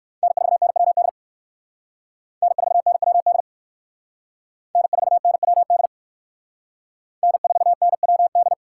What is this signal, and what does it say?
Morse code, heard as a keyed tone, about 56 wpm, 700 Hz. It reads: N4NWD N4NWD N4NWD N4NWD